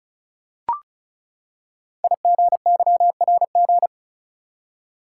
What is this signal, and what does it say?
Morse code, heard as a keyed tone, about 36 wpm, 700 Hz. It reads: IGYRG
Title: Code Groups - 5 Letters - 35wpm